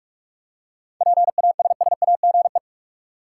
Morse code, heard as a keyed tone, about 42 wpm, 700 Hz. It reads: PASSAGE
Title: Seven Letter Words - 45wpm